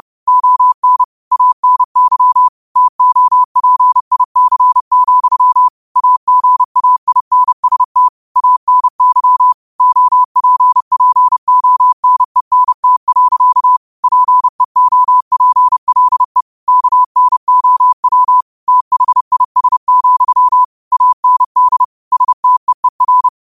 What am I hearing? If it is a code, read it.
Morse code, heard as a keyed tone, about 30 wpm, 1000 Hz. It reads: ON ANY TOPIC, AGAINST ANY OPPONENT. PEOPLE KNOW THIS, AND STEER
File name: qotdfiles / 30wpm